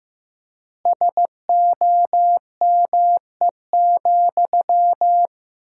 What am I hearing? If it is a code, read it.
Morse code, heard as a keyed tone, about 15 wpm, 700 Hz. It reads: SOME,
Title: 4x Word-Spacing - Sentences from Top 200 Words - 15wpm